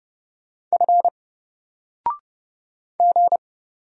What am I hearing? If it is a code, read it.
Morse code, heard as a keyed tone, about 30 wpm, 700 Hz. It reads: F G
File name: Single-Letter-Number-30wpm